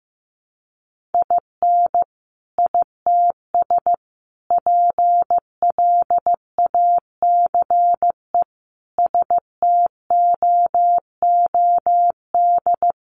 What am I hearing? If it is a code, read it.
Morse code, heard as a keyed tone, about 15 wpm, 700 Hz. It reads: IN ITS PLACE STOOD
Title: Sentences from Top 1400 Words 15wpm